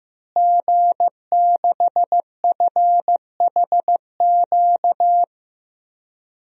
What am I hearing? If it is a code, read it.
Morse code, heard as a keyed tone, about 15 wpm, 700 Hz. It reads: G6FHQ
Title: DXCC-100 Call Signs - 5 Characters - Repeated 3x - 15wpm